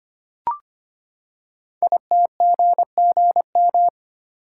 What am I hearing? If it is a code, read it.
Morse code, heard as a keyed tone, about 25 wpm, 700 Hz. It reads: ITGGM
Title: Code Groups - 5 Letters - 25wpm